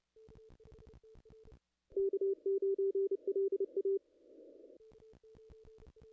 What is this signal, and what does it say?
Morse code, heard as a keyed tone, about 29 wpm, 390 Hz. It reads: K9LA
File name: contests / ss-cw-k5tr-op-n5rz -0000